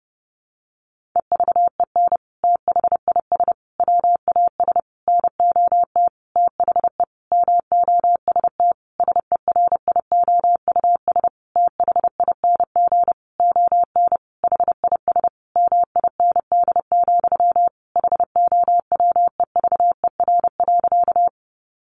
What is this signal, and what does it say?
Morse code, heard as a keyed tone, about 30 wpm, 700 Hz. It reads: EVEN THIS WAS NOT THE MOST SERIOUS THING ON HIS MIND, HOWEVER.
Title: Sentences from Top 1100 Words 30wpm